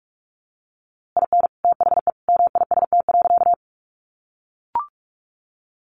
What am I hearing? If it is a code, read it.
Morse code, heard as a keyed tone, about 44 wpm, 700 Hz. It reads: IN THE MIST.
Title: A Sound Of Thunder 45wpm